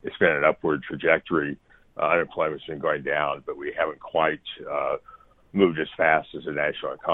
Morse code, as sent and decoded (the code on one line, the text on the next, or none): none